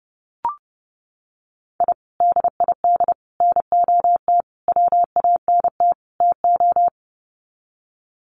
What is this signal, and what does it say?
Morse code, heard as a keyed tone, about 28 wpm, 700 Hz. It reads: I DID NOT WANT TO
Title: Sets of 5 Words Encore 30wpm